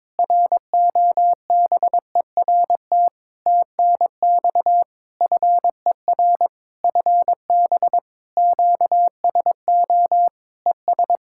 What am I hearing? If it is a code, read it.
Morse code, heard as a keyed tone, about 22 wpm, 700 Hz. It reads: ROBERT TNX FER FB QSO ES
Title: K1USN SST QSO Practice 22wpm